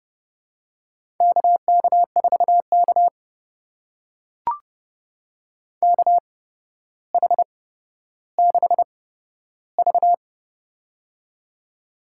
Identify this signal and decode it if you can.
Morse code, heard as a keyed tone, about 30 wpm, spaced as for 13 wpm, 700 Hz. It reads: KK4K KH6V
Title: USA Call Signs - 4 Characters - 8x Character Spacing - 30wpm